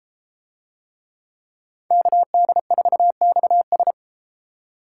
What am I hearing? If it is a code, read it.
Morse code, heard as a keyed tone, about 31 wpm, 700 Hz. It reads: KD4XS